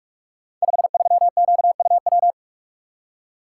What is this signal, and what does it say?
Morse code, heard as a keyed tone, about 44 wpm, 700 Hz. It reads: 53QUW
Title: Code Groups - 5 Letter:Numbers - 45wpm